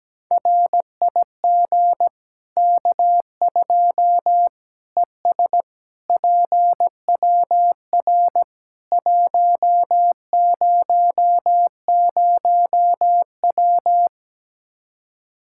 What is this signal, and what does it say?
Morse code, heard as a keyed tone, about 17 wpm, 700 Hz. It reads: RIG K2 ES PWR 100W
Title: Practice QSO - Exchange 2 - Info - 17wpm